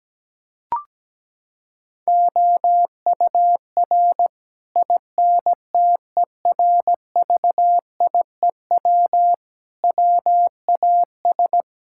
Morse code, heard as a keyed tone, about 17 wpm, 700 Hz. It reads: OUR INTERVIEW WAS
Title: Sentences from Top 1600 Words 17wpm